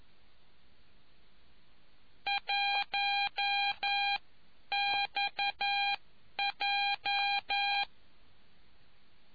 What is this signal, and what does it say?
Morse code, heard as a keyed tone, about 11 wpm, spaced as for 9 wpm, 800 Hz. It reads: 1XJ